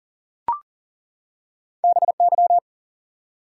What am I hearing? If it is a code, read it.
Morse code, heard as a keyed tone, about 40 wpm, 700 Hz. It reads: BY
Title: Two Letter Words - 40wpm